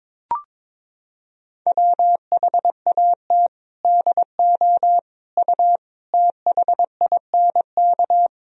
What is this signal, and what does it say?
Morse code, heard as a keyed tone, about 22 wpm, 700 Hz. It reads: WHAT DO U THINK
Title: Sentences from Top 200 Words and Abbreviations 22wpm